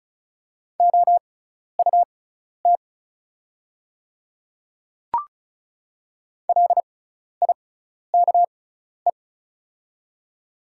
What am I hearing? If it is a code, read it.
Morse code, heard as a keyed tone, about 36 wpm, spaced as for 12 wpm, 690 Hz. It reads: OUT LIKE
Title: 6x Character Spacing - Top 300 Words - 35wpm